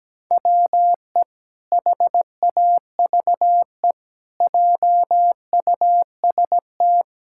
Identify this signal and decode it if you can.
Morse code, heard as a keyed tone, about 17 wpm, 700 Hz. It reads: WE HAVE JUST
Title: Pangram Challenge v2 - 17wpm